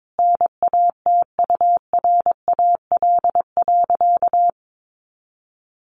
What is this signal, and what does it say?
Morse code, heard as a keyed tone, about 22 wpm, 700 Hz. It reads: NATURAL.
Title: Sentences from Top 700 Words 22wpm